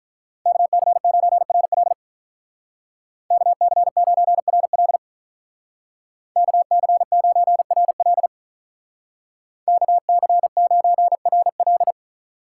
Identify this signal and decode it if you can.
Morse code, heard as a keyed tone, about 45 wpm, 700 Hz. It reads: KC9RL KC9RL KC9RL KC9RL